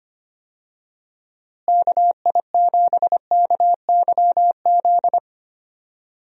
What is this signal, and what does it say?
Morse code, heard as a keyed tone, about 25 wpm, 700 Hz. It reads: KI7KYZ
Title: USA Call Signs - 6 Characters - Repeated 2x - 25wpm